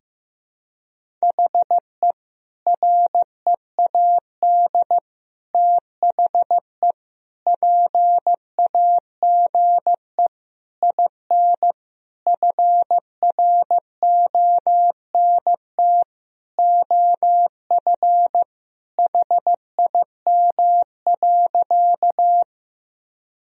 Morse code, heard as a keyed tone, about 15 wpm, 700 Hz. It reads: HE READ THE PAGE IN FRONT OF HIM.
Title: Sentences from Top 1000 Words - Review Part 1 - 15wpm